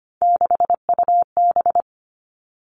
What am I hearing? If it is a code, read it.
Morse code, heard as a keyed tone, about 25 wpm, 700 Hz. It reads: 6UB
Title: All Cut Numbers - 3-Digit 25wpm